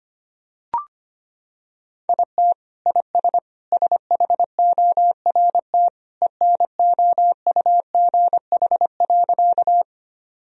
Morse code, heard as a keyed tone, about 25 wpm, 700 Hz. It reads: IT IS SHORT ENOUGH.
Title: Sentences from Top 1000 Words - Review Part 2 - 25wpm